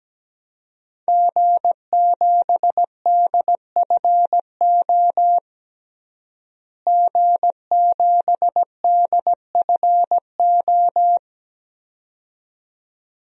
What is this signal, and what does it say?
Morse code, heard as a keyed tone, about 17 wpm, 700 Hz. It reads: G7DFO G7DFO
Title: DXCC-100 Call Signs - 5 Characters - Repeated 2x - 17wpm